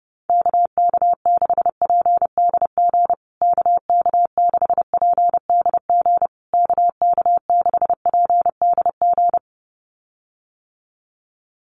Morse code, heard as a keyed tone, about 30 wpm, 700 Hz. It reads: KK6PDG KK6PDG KK6PDG
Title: USA Call Signs Repeated 3x 30wpm